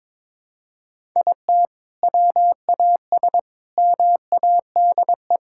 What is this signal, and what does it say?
Morse code, heard as a keyed tone, about 21 wpm, 700 Hz. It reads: IT WAS MADE
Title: Sentences from Top 1000 Words - Review Part 5 - 22wpm